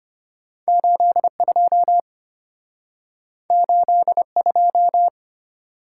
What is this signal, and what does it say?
Morse code, heard as a keyed tone, about 34 wpm, 690 Hz. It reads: 82 82